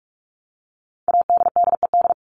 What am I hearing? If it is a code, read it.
Morse code, heard as a keyed tone, about 44 wpm, 700 Hz. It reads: ADDED
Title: ED Suffix - Top 100 Words with Base Words 45wpm